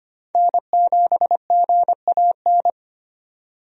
Morse code, heard as a keyed tone, about 25 wpm, 700 Hz. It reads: N7GAN
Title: USA Call Signs - 5 Characters - 25wpm